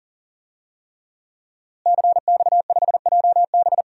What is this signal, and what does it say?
Morse code, heard as a keyed tone, about 40 wpm, 700 Hz. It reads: CX5JB
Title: DXCC-100 Call Signs - 5 Characters - Repeated 3x - 40wpm